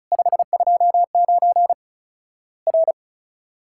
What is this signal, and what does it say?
Morse code, heard as a keyed tone, about 36 wpm, 700 Hz. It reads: 529 R
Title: Worked All Europe DX Contest - QTC Exchange - 35wpm